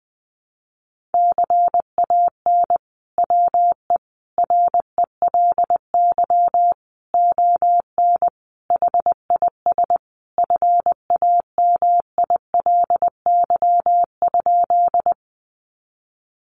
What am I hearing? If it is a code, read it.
Morse code, heard as a keyed tone, about 20 wpm, 700 Hz. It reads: CAN WE RELY ON HIS FAMILY?
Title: Sentences from Top 1200 Words 20wpm